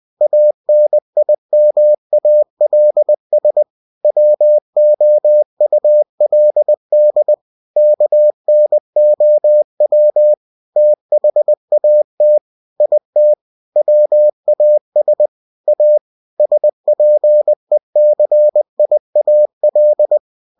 Morse code, heard as a keyed tone, about 20 wpm, 600 Hz. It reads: ANIMALS WOULD KNOW THAT IT WAS A SPECIAL
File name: cw-books / winnie-the-pooh / winnie-20wpm-page-0103